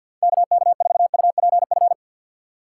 Morse code, heard as a keyed tone, about 53 wpm, 700 Hz. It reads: KK4UPF
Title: USA Call Signs - 2x Character Spacing - 50wpm